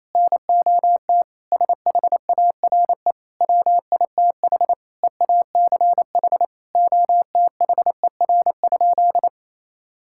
Morse code, heard as a keyed tone, about 28 wpm, 700 Hz. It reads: NOT SHARE WITH EACH OTHER?
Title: Sentences from Top 1000 Words - Review Part 6 - 28wpm